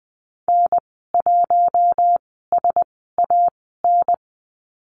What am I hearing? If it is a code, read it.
Morse code, heard as a keyed tone, about 20 wpm, spaced as for 14 wpm, 700 Hz. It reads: N1SAN
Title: 069 Lesson 12g - Letter d - Call Signs 20wpm